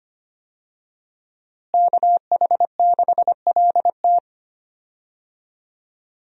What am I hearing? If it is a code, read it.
Morse code, heard as a keyed tone, about 25 wpm, 700 Hz. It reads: KH6LT
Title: USA Call Signs - 5 Characters - Repeated 2x - 25wpm